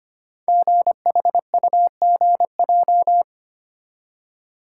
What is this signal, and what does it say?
Morse code, heard as a keyed tone, about 25 wpm, 700 Hz. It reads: GHUGJ
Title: Code Groups - 5 Letters - 25wpm